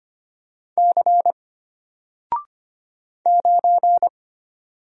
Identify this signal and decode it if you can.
Morse code, heard as a keyed tone, about 25 wpm, 700 Hz. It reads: C 9
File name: Single-Letter-Number-25wpm